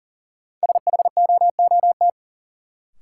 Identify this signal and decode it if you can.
Morse code, heard as a keyed tone, about 40 wpm, 700 Hz. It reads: SHOOT
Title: Sentences from Top 1200 Words - New Word Drill - 40wpm